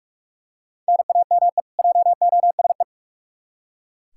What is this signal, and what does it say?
Morse code, heard as a keyed tone, about 44 wpm, 700 Hz. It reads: NAME JOSE
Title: Practice QSO - Exchange 1 - Beginning - 45wpm